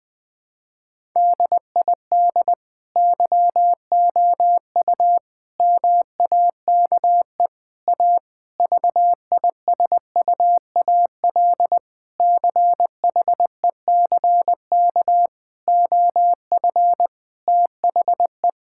Morse code, heard as a keyed tone, about 21 wpm, 700 Hz. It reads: DID YOU MAKE A VISUAL CHECK OF THE
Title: Sentences from Top 1000 Words - Review Part 6 - 20wpm